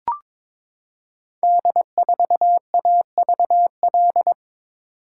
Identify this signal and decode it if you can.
Morse code, heard as a keyed tone, about 22 wpm, 700 Hz. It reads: D4AVL
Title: Code Groups - 5 Letter:Numbers - 22wpm